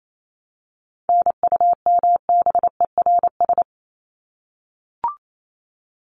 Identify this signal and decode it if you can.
Morse code, heard as a keyed tone, about 28 wpm, 700 Hz. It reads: NUMBERS
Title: ES:S Suffix - Top 100 Words with Base Words 28wpm